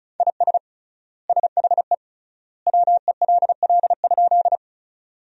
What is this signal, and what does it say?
Morse code, heard as a keyed tone, about 36 wpm, 700 Hz. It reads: IS SHE WELL?
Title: 3x Word-Spacing - Sentences from Top 200 Words - 35wpm